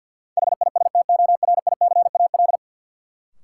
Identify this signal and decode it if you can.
Morse code, heard as a keyed tone, about 52 wpm, 700 Hz. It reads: HISTORICAL
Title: Sentences from Top 600 Words - New Word Drill - 50wpm